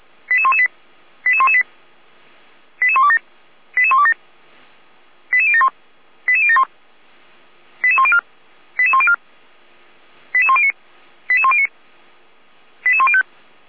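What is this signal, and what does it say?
Five-tone selective call (ZVEI), heard as quick runs of five tones, tones of about 70 ms. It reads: 88118 88118 88127 88127 88071 88071 88104 88104 88109 88109 88105
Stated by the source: ZVEI five-tone call 88090 received by SDR.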